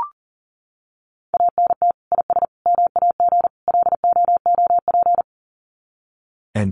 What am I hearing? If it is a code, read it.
Morse code, heard as a keyed tone, about 40 wpm, 700 Hz. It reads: ANT IS MAG LOOP
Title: Common QSO Phrases 40wpm